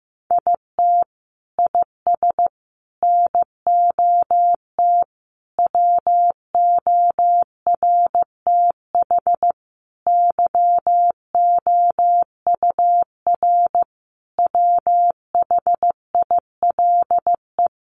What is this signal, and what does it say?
Morse code, heard as a keyed tone, about 15 wpm, 700 Hz. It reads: IT IS NOT WORTH YOUR WHILE
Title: Sentences from Top 800 Words 15wpm